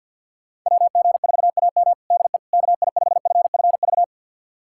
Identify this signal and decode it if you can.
Morse code, heard as a keyed tone, about 51 wpm, 700 Hz. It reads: WG4AM DE KI5FFV
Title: USA Callsign DE Callsign 50wpm